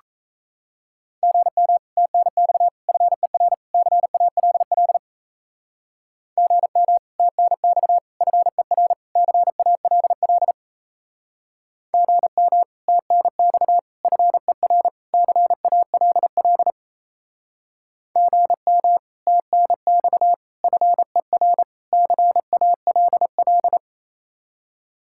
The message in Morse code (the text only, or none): GM TNX FER CALL GM TNX FER CALL GM TNX FER CALL GM TNX FER CALL